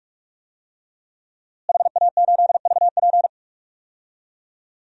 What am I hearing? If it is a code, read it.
Morse code, heard as a keyed tone, about 45 wpm, 700 Hz. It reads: HA8VP
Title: DXCC-100 Call Signs - 5 Characters - Repeated 3x - 45wpm